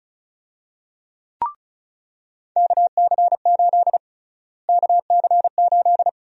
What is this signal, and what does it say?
Morse code, heard as a keyed tone, about 34 wpm, 700 Hz. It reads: KC8 KC8
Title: USA Call Sign Prefixes - Repeated 2x - 35wpm